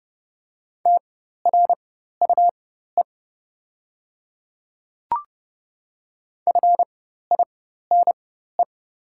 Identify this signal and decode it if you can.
Morse code, heard as a keyed tone, about 30 wpm, spaced as for 14 wpm, 700 Hz. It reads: TRUE FINE
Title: 4x Character Spacing - Top 500 Words - 30wpm